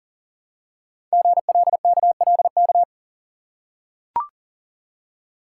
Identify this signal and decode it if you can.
Morse code, heard as a keyed tone, about 41 wpm, 700 Hz. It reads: GLKLK
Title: Code Groups - 5 Letters - 40wpm